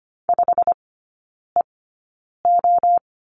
Morse code, heard as a keyed tone, about 25 wpm, spaced as for 9 wpm, 690 Hz. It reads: 5EO
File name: Lesson 15e - Number 5 - Three Characters 25wpm